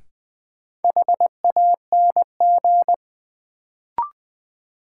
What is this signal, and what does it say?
Morse code, heard as a keyed tone, about 20 wpm, 700 Hz. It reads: HANG
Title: Sentences from Top 1600 Words - New Word Drill Alt - 20wpm